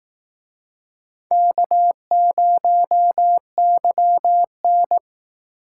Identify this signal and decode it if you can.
Morse code, heard as a keyed tone, about 19 wpm, 700 Hz. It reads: K0YN
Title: Speed Racing - USA 1x2 Call Signs 15wpm